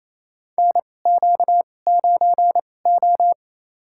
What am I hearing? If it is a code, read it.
Morse code, heard as a keyed tone, about 28 wpm, spaced as for 20 wpm, 700 Hz. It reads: NQ9O
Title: USA Call Signs - 4 Characters - 2x Character Spacing - 28wpm